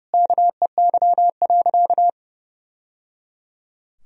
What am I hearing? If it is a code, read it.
Morse code, heard as a keyed tone, about 30 wpm, 700 Hz. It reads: KEY.
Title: Sentences from Top 1000 Words - Review Part 4 - 30wpm